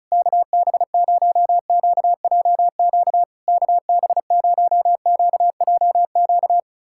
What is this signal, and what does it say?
Morse code, heard as a keyed tone, about 34 wpm, 690 Hz. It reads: KB0QJQ KB0QJQ
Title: SOTA QSO Practice 35wpm